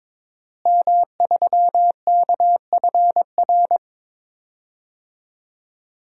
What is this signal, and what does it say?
Morse code, heard as a keyed tone, about 21 wpm, 700 Hz. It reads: M3KFR
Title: DXCC-100 Call Signs - 5 Characters - Repeated 2x - 22wpm